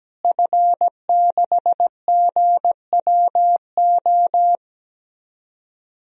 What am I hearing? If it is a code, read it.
Morse code, heard as a keyed tone, about 17 wpm, 700 Hz. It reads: F6GWO
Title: DXCC-100 Call Signs - 5 Characters - Repeated 3x - 17wpm